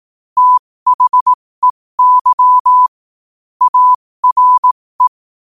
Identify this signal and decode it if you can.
Morse code, heard as a keyed tone, about 18 wpm, spaced as for 15 wpm, 1000 Hz. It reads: THEY ARE
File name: qotdfiles / 15wpm